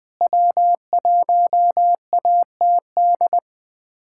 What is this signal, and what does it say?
Morse code, heard as a keyed tone, about 20 wpm, 700 Hz. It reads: W1ATD
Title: USA Callsign DE Callsign - 5 Characters - 20wpm